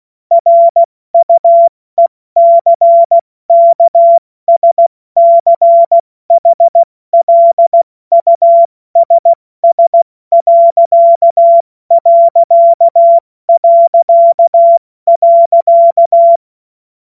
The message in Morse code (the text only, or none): RUECKSCHLUSS....